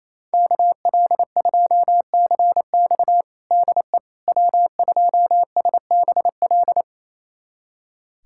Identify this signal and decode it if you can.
Morse code, heard as a keyed tone, about 28 wpm, 700 Hz. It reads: KL2CX DE W2SBL
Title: USA Callsign DE Callsign - 5 Characters - 28wpm